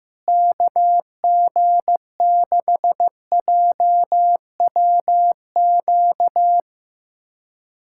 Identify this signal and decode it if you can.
Morse code, heard as a keyed tone, about 15 wpm, 700 Hz. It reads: KG6JWQ
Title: USA Callsign DE Callsign - 6 Characters - 15wpm